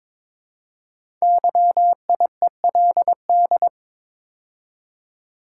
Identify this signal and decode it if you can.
Morse code, heard as a keyed tone, about 22 wpm, 700 Hz. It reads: YIELD
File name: 5 Letter Words - First 3 Letters Given - 22wpm